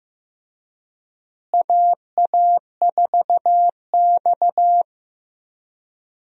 Morse code, heard as a keyed tone, about 15 wpm, 700 Hz. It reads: AA4X
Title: USA Call Signs - 4 Characters - Repeated 3x - 15wpm